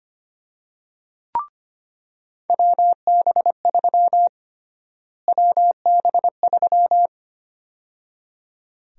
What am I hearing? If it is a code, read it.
Morse code, heard as a keyed tone, about 25 wpm, 700 Hz. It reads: WB3 WB3
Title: USA Call Sign Prefixes - Repeated 2x - 25wpm